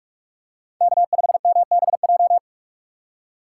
Morse code, heard as a keyed tone, about 44 wpm, 700 Hz. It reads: K5MBJ
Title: USA Call Signs - 5 Characters - 3x Character Spacing - 45wpm